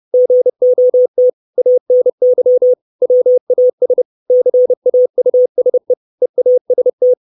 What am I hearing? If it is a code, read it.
Morse code, heard as a keyed tone, about 30 wpm, 500 Hz. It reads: GOT ANY WAS CAUSE EAST
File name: 632 Words 30WPM